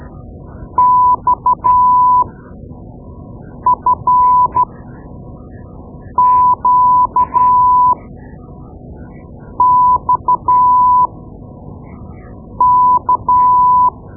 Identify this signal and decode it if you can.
Morse code, heard as a keyed tone, about 12 wpm, spaced as for 5 wpm, 1000 Hz. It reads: XFQXK
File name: Morse Lessons / Morse Lessons / Tape-2B